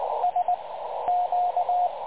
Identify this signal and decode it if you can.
Morse code, heard as a keyed tone, about 20 wpm, 720 Hz, with radio noise. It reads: S Q
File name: dlazaawansowanych / nagraniazpasma / websdr recording 3567.0kHz